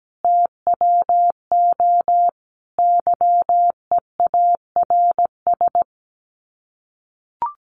Morse code, heard as a keyed tone, about 17 wpm, 700 Hz. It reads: TWO YEARS
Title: Sets of 5 Words 17wpm